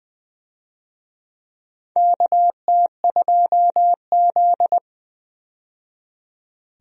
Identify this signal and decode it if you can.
Morse code, heard as a keyed tone, about 20 wpm, 700 Hz. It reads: KT2Z
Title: USA Call Signs - 4 Characters - 20wpm